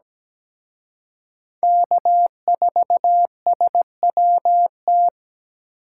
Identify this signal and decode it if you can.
Morse code, heard as a keyed tone, about 20 wpm, 700 Hz. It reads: K4SWT
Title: Speed Racing - USA 1x3 Call Signs 17wpm